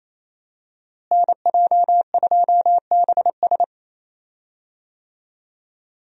Morse code, heard as a keyed tone, about 27 wpm, 700 Hz. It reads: NJ2BS
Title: USA Call Signs - 5 Characters - Repeated 3x - 28wpm